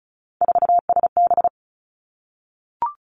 Morse code, heard as a keyed tone, about 36 wpm, 700 Hz. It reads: 4SB
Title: Three Challenging Letter-Number Combinations 35wpm